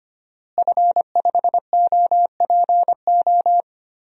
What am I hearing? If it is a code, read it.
Morse code, heard as a keyed tone, about 25 wpm, 700 Hz. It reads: F5OPO
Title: DXCC-100 Call Signs - 5 Characters - Repeated 3x - 25wpm